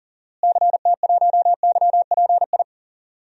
Morse code, heard as a keyed tone, about 37 wpm, 700 Hz. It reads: CT1YPI
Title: DXCC-100 Call Signs - 6 Characters - Repeated 3x - 40wpm